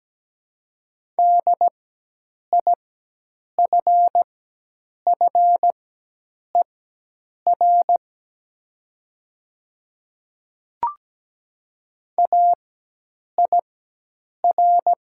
Morse code, heard as a keyed tone, about 17 wpm, spaced as for 8 wpm, 700 Hz. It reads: DIFFER AIR
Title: 4x Character Spacing - Top 300 Words - 17wpm